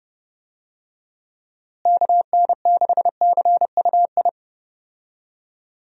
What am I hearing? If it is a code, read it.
Morse code, heard as a keyed tone, about 32 wpm, 700 Hz. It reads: KN6CUI
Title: USA Call Signs - 6 Characters - Repeated 3x - 30wpm